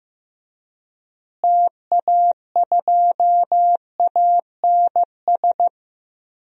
Morse code, heard as a keyed tone, about 15 wpm, 700 Hz. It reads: TA2ANS
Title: DXCC-100 Call Signs - 6 Characters - Repeated 3x - 15wpm